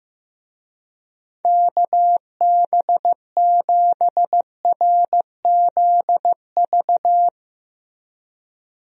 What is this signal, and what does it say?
Morse code, heard as a keyed tone, about 15 wpm, 700 Hz. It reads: KB7RZV
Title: USA Call Signs - 6 Characters - Repeated 2x - 15wpm